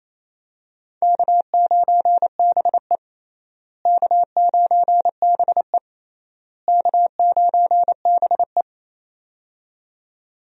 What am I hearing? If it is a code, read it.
Morse code, heard as a keyed tone, about 27 wpm, 700 Hz. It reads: K9BE K9BE K9BE
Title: USA Call Signs - 4 Characters - Repeated 3x - 28wpm